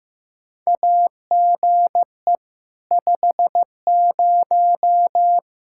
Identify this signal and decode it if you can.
Morse code, heard as a keyed tone, about 15 wpm, 700 Hz. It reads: AGE 50
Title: Practice QSO - Exchange 3 - More Info - 15wpm